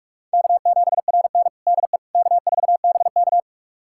Morse code, heard as a keyed tone, about 46 wpm, 700 Hz. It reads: K7RN DE K4BK
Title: USA Callsign DE Callsign - 4 Characters - 45wpm